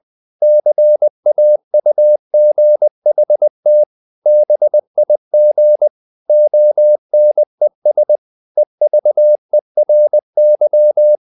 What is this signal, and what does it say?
Morse code, heard as a keyed tone, about 20 wpm, 600 Hz. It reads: CAUGHT BIG ONES EVERY
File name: the old man and the sea / the old man and the sea part 1 20wpm